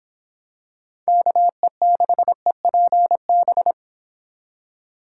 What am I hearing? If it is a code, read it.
Morse code, heard as a keyed tone, about 25 wpm, 700 Hz. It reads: KE6EPB